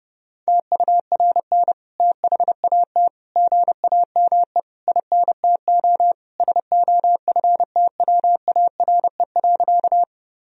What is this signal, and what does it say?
Morse code, heard as a keyed tone, about 32 wpm, 700 Hz. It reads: TURN THAT GAME INTO SOFTWARE.
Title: Sentences from Top 1000 Words - Review Part 3 - 30wpm